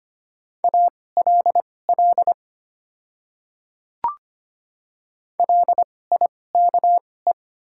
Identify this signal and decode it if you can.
Morse code, heard as a keyed tone, about 25 wpm, spaced as for 18 wpm, 700 Hz. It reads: ALL LIKE
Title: 2x Character Spacing - Top 100 Words - 25wpm